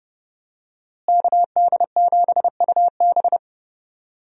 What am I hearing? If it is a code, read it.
Morse code, heard as a keyed tone, about 30 wpm, 700 Hz. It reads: KD7UB
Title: USA Call Signs - 5 Characters - Repeated 2x - 30wpm